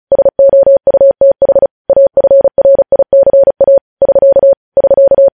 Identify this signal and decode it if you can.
Morse code, heard as a keyed tone, about 36 wpm, 550 Hz. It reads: SOUTH AFRICA <SK> <SK>
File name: podcast / News headlines-1x-35x35